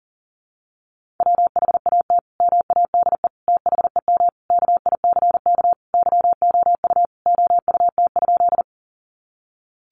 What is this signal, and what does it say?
Morse code, heard as a keyed tone, about 40 wpm, 700 Hz. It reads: WHAT MADE THEM KICK YOU OUT?
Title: Sentences from Top 1300 Words 40wpm